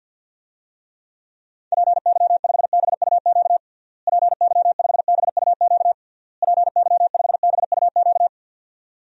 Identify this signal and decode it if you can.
Morse code, heard as a keyed tone, about 51 wpm, 700 Hz. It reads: PY5BUQ PY5BUQ PY5BUQ
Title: DXCC-100 Call Signs - 6 Characters - Repeated 3x - 50wpm